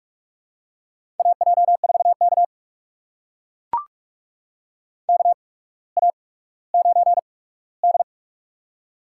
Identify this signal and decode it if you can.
Morse code, heard as a keyed tone, about 44 wpm, spaced as for 19 wpm, 700 Hz. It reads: AJ4K KA9D
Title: USA Call Signs - 4 Characters - 8x Character Spacing - 45wpm